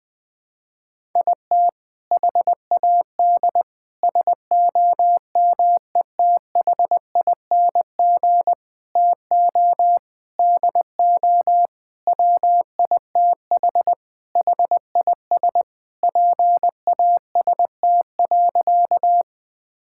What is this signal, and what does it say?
Morse code, heard as a keyed tone, about 21 wpm, 700 Hz. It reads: IT HAD SOMETHING TO DO WITH HIS PAST.
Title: Sentences with 5-Word Phrases - Spoken Beginning - 20wpm